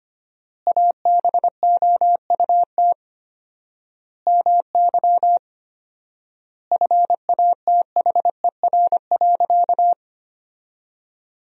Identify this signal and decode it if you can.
Morse code, heard as a keyed tone, about 25 wpm, 700 Hz. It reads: ABOUT MY FATHER.
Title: 4x Word-Spacing - Sentences from Top 400 Words - 25wpm